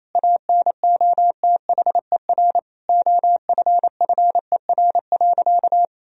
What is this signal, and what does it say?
Morse code, heard as a keyed tone, about 28 wpm, 700 Hz. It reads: ANOTHER OFFER.
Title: Sentences from Top 1000 Words - Review Part 4 - 28wpm